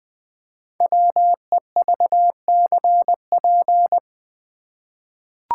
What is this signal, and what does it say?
Morse code, heard as a keyed tone, about 20 wpm, 700 Hz. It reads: WEVCP
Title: Code Groups - 5 Letter:Numbers - 20wpm